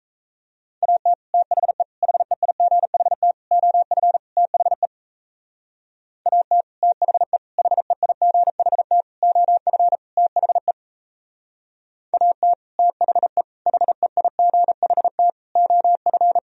Morse code, heard as a keyed tone, about 34 wpm, 700 Hz. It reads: AT THE HEIGHT OF THE AT THE HEIGHT OF THE AT THE HEIGHT OF
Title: Speed-Racing - Sets of 5 Words 28wpm